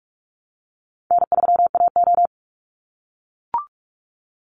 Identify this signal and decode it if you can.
Morse code, heard as a keyed tone, about 44 wpm, 700 Hz. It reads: N3AO
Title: Easy Call Signs - 45wpm